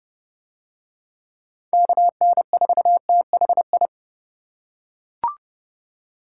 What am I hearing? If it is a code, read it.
Morse code, heard as a keyed tone, about 29 wpm, 700 Hz. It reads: KN4THI